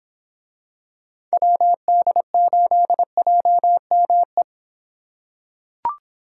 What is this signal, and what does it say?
Morse code, heard as a keyed tone, about 24 wpm, 700 Hz. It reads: WD8JME